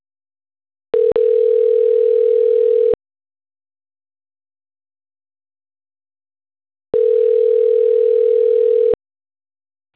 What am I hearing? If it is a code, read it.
Ringback tone (North America), 440 + 480 Hz.